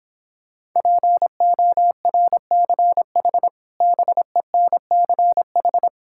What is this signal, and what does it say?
Morse code, heard as a keyed tone, about 26 wpm, 700 Hz. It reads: PORCH BENCH